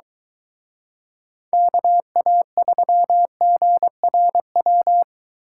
Morse code, heard as a keyed tone, about 22 wpm, 700 Hz. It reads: KA3GRW